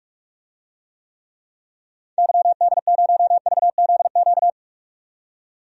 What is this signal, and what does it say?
Morse code, heard as a keyed tone, about 41 wpm, 700 Hz. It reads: YD0VZQ
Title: DXCC-100 Call Signs - 6 Characters - 45wpm